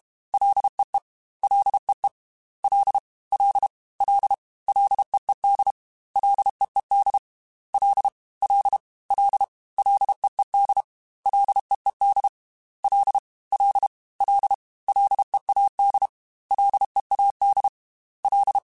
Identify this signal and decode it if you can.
Morse code, heard as a keyed tone, about 32 wpm, 800 Hz. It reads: LEE LEE L L L LEED LEED L L L LEED LEED L L L LEAD LEAD L
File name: F and L / F and L mp3 / F and L Prefixes 3x2 Twice @30